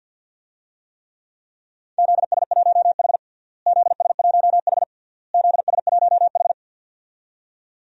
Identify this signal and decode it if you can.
Morse code, heard as a keyed tone, about 51 wpm, 700 Hz. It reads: ZS1H ZS1H ZS1H
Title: DXCC-100 Call Signs - 4 Characters - Repeated 3x - 50wpm